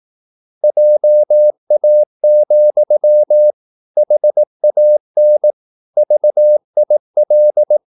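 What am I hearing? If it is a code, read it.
Morse code, heard as a keyed tone, about 18 wpm, 600 Hz. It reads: JA, HAN VIL
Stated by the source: Slow (QRS) Morse practice at 17 wpm, from CW med Gnister.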